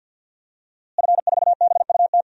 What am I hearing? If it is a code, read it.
Morse code, heard as a keyed tone, about 52 wpm, 700 Hz. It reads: F4BUT
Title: DXCC-100 Call Signs - 5 Characters - Repeated 3x - 50wpm